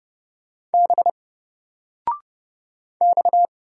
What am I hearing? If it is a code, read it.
Morse code, heard as a keyed tone, about 31 wpm, 700 Hz. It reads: B X
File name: Single-Letter-Number-30wpm